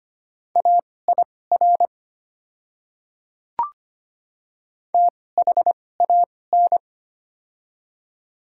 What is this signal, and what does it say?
Morse code, heard as a keyed tone, about 25 wpm, spaced as for 18 wpm, 700 Hz. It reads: AIR THAN
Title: 2x Character Spacing - Top 200 Words - 25wpm